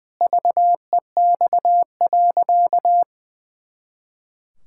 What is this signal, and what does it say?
Morse code, heard as a keyed tone, about 21 wpm, 700 Hz. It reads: VEX.
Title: Pangram Challenge v2 - 20wpm